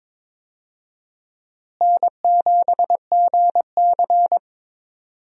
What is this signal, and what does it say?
Morse code, heard as a keyed tone, about 22 wpm, 700 Hz. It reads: N7GC
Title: USA Call Signs - 4 Characters - Repeated 2x - 22wpm